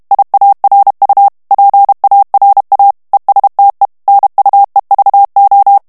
Morse code, heard as a keyed tone, about 32 wpm, 800 Hz. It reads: IARU PARA ESTE NUEVO
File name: morse mp3 / ESPAGNOL TEXTE EN CLAIR 1 VITESSE 30 MOTS MINUTES